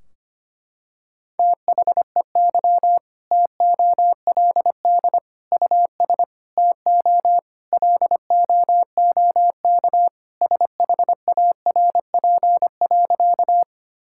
Morse code, heard as a keyed tone, about 25 wpm, 700 Hz. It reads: THEY TOLD US TO LOOK SHARP.
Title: Sentences from Top 1600 Words 25wpm